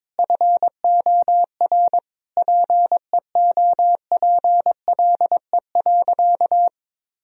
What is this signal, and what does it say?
Morse code, heard as a keyed tone, about 22 wpm, 700 Hz. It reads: FOR PEOPLE.
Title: Sentences from Top 1000 Words - Review Part 3 - 22wpm